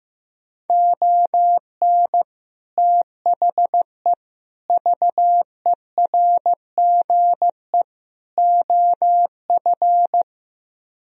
Morse code, heard as a keyed tone, about 15 wpm, 700 Hz. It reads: ON THE VERGE OF
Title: Speed-Racing - Sets of 4 Words 15wpm